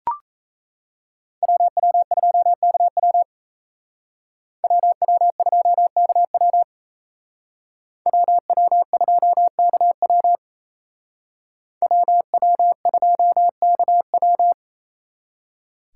Morse code, heard as a keyed tone, about 35 wpm, 700 Hz. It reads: WW2KW WW2KW WW2KW WW2KW
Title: Speed Racing - USA 2x2 Call Signs 28wpm